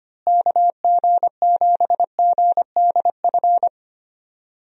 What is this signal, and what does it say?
Morse code, heard as a keyed tone, about 25 wpm, 700 Hz. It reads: KG7GDF
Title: USA Call Signs - 6 Characters - Repeated 3x - 25wpm